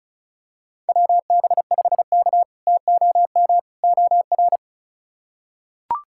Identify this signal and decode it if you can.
Morse code, heard as a keyed tone, about 34 wpm, 700 Hz. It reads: WB5K TOM OR
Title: K1USN SST QSO Practice - 35wpm